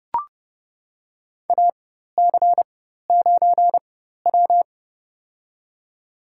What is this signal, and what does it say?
Morse code, heard as a keyed tone, about 30 wpm, spaced as for 14 wpm, 700 Hz. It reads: AC9W
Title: USA Call Signs - 4 Characters - 4x Character Spacing - 30wpm